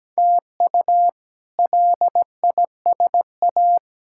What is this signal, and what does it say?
Morse code, heard as a keyed tone, about 17 wpm, 700 Hz. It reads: TU LISA
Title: K1USN SST QSO Practice - 17wpm